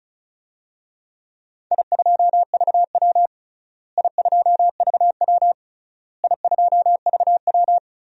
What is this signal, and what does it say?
Morse code, heard as a keyed tone, about 34 wpm, 700 Hz. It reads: I2VW I2VW I2VW
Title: DXCC-100 Call Signs - 4 Characters - Repeated 3x - 35wpm